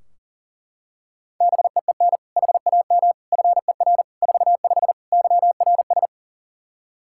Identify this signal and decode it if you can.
Morse code, heard as a keyed tone, about 40 wpm, 700 Hz. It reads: BEEN HAM FER 45 YRS
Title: Practice QSO - Exchange 3 - More Info - 40wpm